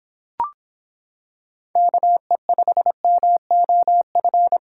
Morse code, heard as a keyed tone, about 26 wpm, 700 Hz. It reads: KE5MOF